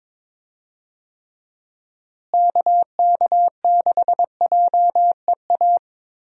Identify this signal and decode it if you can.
Morse code, heard as a keyed tone, about 22 wpm, 700 Hz. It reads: KK6JEA
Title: USA Call Signs - 6 Characters - Repeated 2x - 22wpm